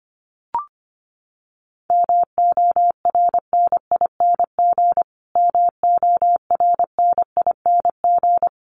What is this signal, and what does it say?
Morse code, heard as a keyed tone, about 25 wpm, 700 Hz. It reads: MORNING MORNING
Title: Top 500 Words Repeated 25wpm